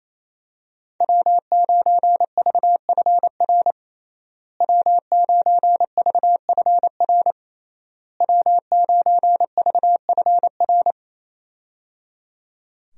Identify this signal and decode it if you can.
Morse code, heard as a keyed tone, about 28 wpm, 700 Hz. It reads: W9VFR W9VFR W9VFR
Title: USA Call Signs - 5 Characters - Repeated 3x - 28wpm